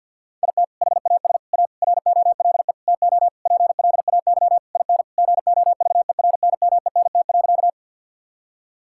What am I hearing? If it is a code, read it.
Morse code, heard as a keyed tone, about 50 wpm, 700 Hz. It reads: IT HAS A ROLE TO PLAY IN GOVERNMENT.
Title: Sentences from Top 1000 Words - Review Part 6 - 50wpm